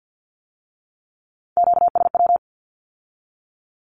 Morse code, heard as a keyed tone, about 50 wpm, 700 Hz. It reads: QSW
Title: Q-Codes with Spoken Meaning - 50wpm